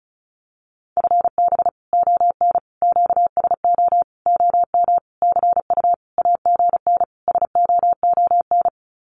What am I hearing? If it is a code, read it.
Morse code, heard as a keyed tone, about 34 wpm, 700 Hz. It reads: FB ON QSO OM CU AGN SOON
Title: Common QSO Phrases 35wpm